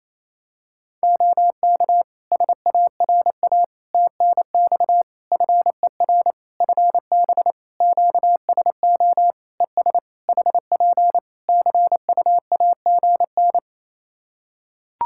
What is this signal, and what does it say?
Morse code, heard as a keyed tone, about 28 wpm, 700 Hz. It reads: OK SARA TNX FER FB QSO ES HP CUAGN
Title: Practice QSO - Exchange 4 - Ending - 28wpm